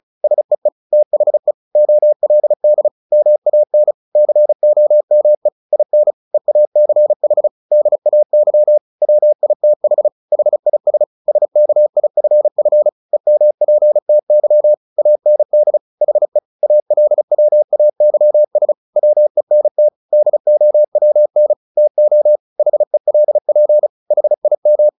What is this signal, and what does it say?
Morse code, heard as a keyed tone, about 34 wpm, 600 Hz. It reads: SEE THE OLD MAN COME IN EACH DAY WITH HIS SKIFF EMPTY AND HE ALWAYS WENT DOWN TO HELP HIM
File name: the old man and the sea / the old man and the sea 1 35wpm